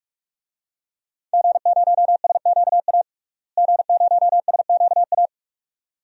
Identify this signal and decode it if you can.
Morse code, heard as a keyed tone, about 42 wpm, 700 Hz. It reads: G0SQA G0SQA
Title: DXCC-100 Call Signs - 5 Characters - Repeated 2x - 45wpm